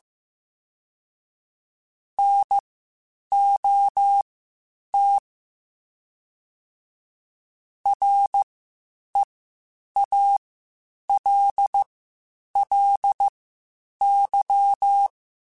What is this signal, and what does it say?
Morse code, heard as a keyed tone, about 15 wpm, spaced as for 8 wpm, 800 Hz. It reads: NOT REALLY